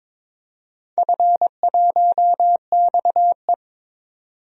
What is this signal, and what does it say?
Morse code, heard as a keyed tone, about 22 wpm, 700 Hz. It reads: F1XE
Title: DXCC-100 Call Signs - 4 Characters - 22wpm